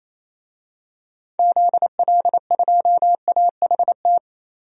Morse code, heard as a keyed tone, about 28 wpm, 700 Hz. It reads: ZL2AHT
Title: DXCC-100 Call Signs - 6 Characters - 28wpm